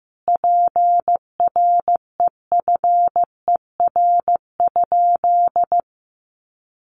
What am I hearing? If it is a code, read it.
Morse code, heard as a keyed tone, about 15 wpm, 700 Hz. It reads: PREFER?
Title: Sentences from Top 1100 Words 15wpm